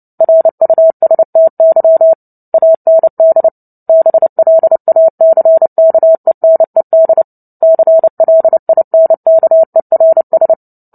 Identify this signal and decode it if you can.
Morse code, heard as a keyed tone, about 30 wpm, 650 Hz. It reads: RUSTY AND BLACKENED CLINKERS
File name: Book Two / 03-Chapter 3 - 29 WPM